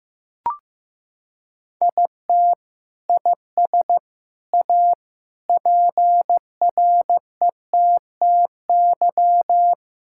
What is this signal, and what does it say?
Morse code, heard as a keyed tone, about 15 wpm, 700 Hz. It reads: IT IS A PRETTY
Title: Sentences from Top 1000 Words - Review Part 6 - 15wpm